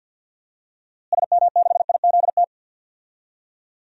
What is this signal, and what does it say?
Morse code, heard as a keyed tone, about 48 wpm, 700 Hz. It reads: SM6IZT
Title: DXCC-100 Call Signs - 6 Characters - Repeated 3x - 50wpm